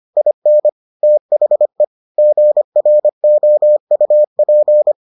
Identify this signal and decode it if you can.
Morse code, heard as a keyed tone, about 25 wpm, 600 Hz. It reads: IN THE GROUP